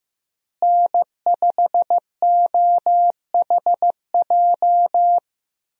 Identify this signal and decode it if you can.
Morse code, heard as a keyed tone, about 15 wpm, 700 Hz. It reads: N5OHJ
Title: USA Call Signs - 5 Characters - Repeated 3x - 15wpm